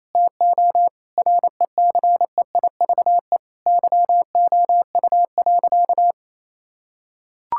Morse code, heard as a keyed tone, about 28 wpm, 700 Hz. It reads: TO RECEIVE YOU.
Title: Sentences from Top 1600 Words 28wpm